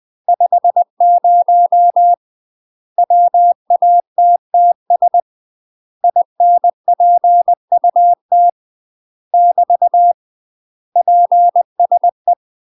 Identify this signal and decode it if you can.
Morse code, heard as a keyed tone, about 20 wpm, 700 Hz. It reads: 50 WATTS INPUT <BT> PSE
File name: UK MASTER QSOs TRAINING / 20WPM / UK MASTER QSOs 0052